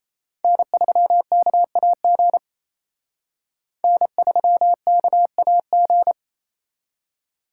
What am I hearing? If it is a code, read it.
Morse code, heard as a keyed tone, about 35 wpm, 700 Hz. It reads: N3KAG N3KAG